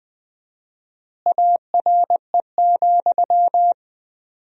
Morse code, heard as a keyed tone, about 20 wpm, 700 Hz. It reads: ARE,
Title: 4x Word-Spacing - Sentences from Top 100 Words - 20wpm